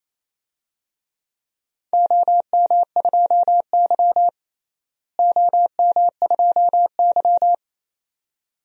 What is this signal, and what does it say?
Morse code, heard as a keyed tone, about 28 wpm, 700 Hz. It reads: OM2Y OM2Y
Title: DXCC-100 Call Signs - 4 Characters - Repeated 2x - 28wpm